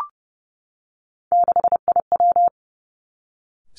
Morse code, heard as a keyed tone, about 30 wpm, 700 Hz. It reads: 6IW
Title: Three Challenging Letter-Number Combinations 30wpm